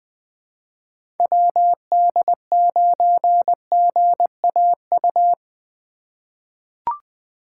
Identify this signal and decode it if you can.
Morse code, heard as a keyed tone, about 20 wpm, 700 Hz. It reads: WD9GAU
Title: USA Call Signs - 6 Characters - 2x Character Spacing - 20wpm